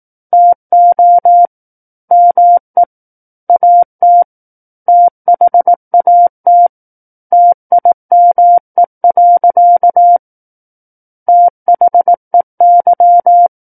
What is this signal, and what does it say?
Morse code, heard as a keyed tone, about 18 wpm, 700 Hz. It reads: TO ME AT THAT TIME. THEY
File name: 18 WPM